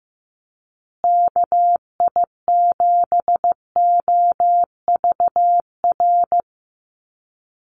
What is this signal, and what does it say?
Morse code, heard as a keyed tone, about 15 wpm, 700 Hz. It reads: KI7OVR
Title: USA Call Signs Repeated 3x 15wpm